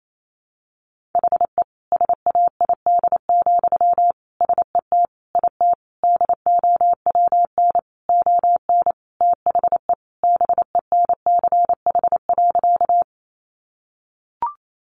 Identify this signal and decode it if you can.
Morse code, heard as a keyed tone, about 28 wpm, 700 Hz. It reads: HE SAID, SET IT DOWN ON THE BENCH.
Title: Sentences from Top 1400 Words 28wpm